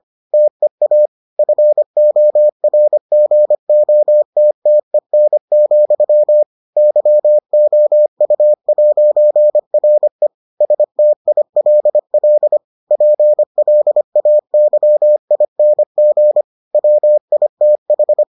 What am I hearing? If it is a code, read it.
Morse code, heard as a keyed tone, about 25 wpm, 600 Hz. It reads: TEA FORGOTTEN, YOU'RE STILL PLAYING WITH